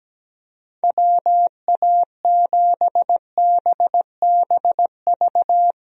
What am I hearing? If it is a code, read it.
Morse code, heard as a keyed tone, about 17 wpm, 700 Hz. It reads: WA7BBV
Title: USA Call Signs - 6 Characters - Repeated 3x - 17wpm